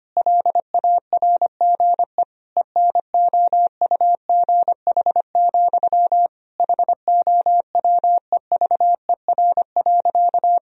Morse code, heard as a keyed tone, about 25 wpm, 700 Hz. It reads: LARGE ENOUGH, HOWEVER.
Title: Sentences from Top 1000 Words - Review Part 1 - 25wpm